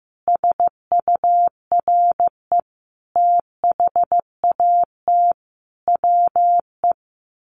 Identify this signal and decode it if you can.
Morse code, heard as a keyed tone, about 15 wpm, 700 Hz. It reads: SURE THAT WE
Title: Sets of 5 Words Encore 15wpm